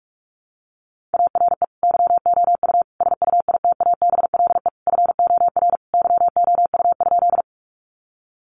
Morse code, heard as a keyed tone, about 44 wpm, 700 Hz. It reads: ARE YOU SUITABLE FOR YOU?
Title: Sentences from Top 1400 Words 45wpm